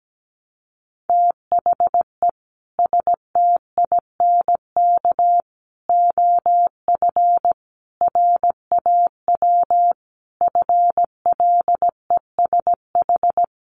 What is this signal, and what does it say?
Morse code, heard as a keyed tone, about 17 wpm, 700 Hz. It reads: THE STINK OF RAW FLESH
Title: A Sound Of Thunder 17wpm